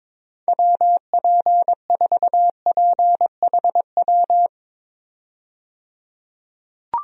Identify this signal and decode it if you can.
Morse code, heard as a keyed tone, about 21 wpm, 700 Hz. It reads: WP4PHW
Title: USA Call Signs - 6 Characters - 22wpm